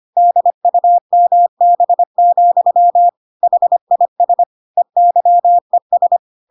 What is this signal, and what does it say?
Morse code, heard as a keyed tone, about 25 wpm, 700 Hz. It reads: DUMB, HIS EYES
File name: Pinocchio / Pinocchio ch1 25 25 ik1hge